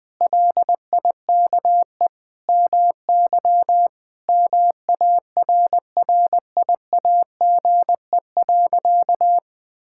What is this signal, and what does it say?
Morse code, heard as a keyed tone, about 20 wpm, 700 Hz. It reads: LIKE MY MARRIAGE.
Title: Sentences from Top 1000 Words - Review Part 5 - 20wpm